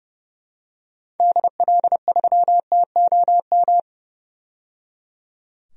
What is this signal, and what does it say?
Morse code, heard as a keyed tone, about 30 wpm, 700 Hz. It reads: DL3TOM
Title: DXCC-100 Call Signs - 6 Characters - 30wpm